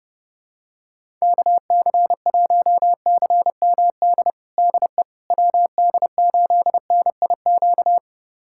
Morse code, heard as a keyed tone, about 32 wpm, 700 Hz. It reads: KC1CMD DE WD8NIQ
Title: USA Callsign DE Callsign 30wpm